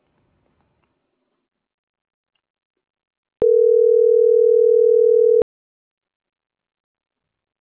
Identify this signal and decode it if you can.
Ringback tone (North America), 440 + 480 Hz.